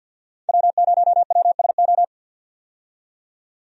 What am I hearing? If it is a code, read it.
Morse code, heard as a keyed tone, about 50 wpm, 700 Hz. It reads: W0WSO
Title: USA Call Signs - 5 Characters - 50wpm